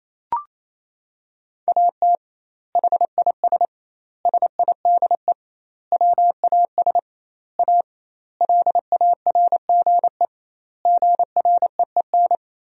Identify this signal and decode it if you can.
Morse code, heard as a keyed tone, about 28 wpm, 700 Hz. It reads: AT HIS SIDE WAS A LARGE GREEN
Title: 2x Word-Spacing - Sentences from Top 500 Words - 28wpm